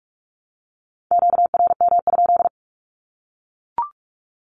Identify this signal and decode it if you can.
Morse code, heard as a keyed tone, about 44 wpm, 700 Hz. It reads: QRM?
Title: Q-Codes 45wpm